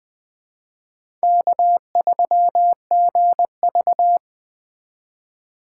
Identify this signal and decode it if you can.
Morse code, heard as a keyed tone, about 20 wpm, 700 Hz. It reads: K3GV
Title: Speed Racing - USA 1x2 Call Signs 17wpm